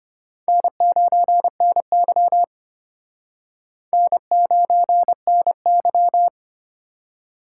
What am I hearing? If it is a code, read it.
Morse code, heard as a keyed tone, about 31 wpm, 700 Hz. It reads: N9NY N9NY
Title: Speed Racing - USA Call Signs 25wpm